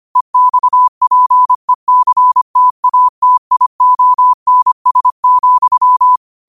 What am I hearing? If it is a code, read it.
Morse code, heard as a keyed tone, about 25 wpm, 1000 Hz. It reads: EXPECTATIONS,
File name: qotdfiles / 25wpm